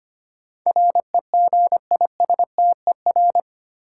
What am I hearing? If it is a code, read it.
Morse code, heard as a keyed tone, about 25 wpm, 700 Hz. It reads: REGISTER
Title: Sentences from Top 1200 Words - New Word Drill - 25wpm